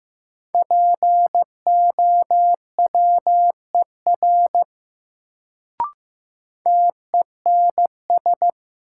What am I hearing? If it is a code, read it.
Morse code, heard as a keyed tone, about 15 wpm, 700 Hz. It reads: POWER TENS
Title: Sets of 4 Words - New Word Drill - 15wpm